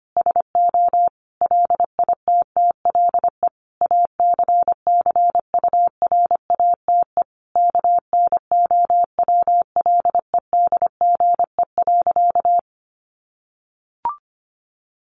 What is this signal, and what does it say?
Morse code, heard as a keyed tone, about 25 wpm, 700 Hz. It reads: SO LITTLE ACCURATE KNOWLEDGE.
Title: Sentences from Top 1200 Words 25wpm